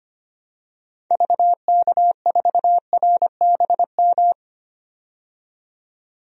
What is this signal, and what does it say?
Morse code, heard as a keyed tone, about 25 wpm, 700 Hz. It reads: VK4RBM
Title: DXCC-100 Call Signs - 6 Characters - 25wpm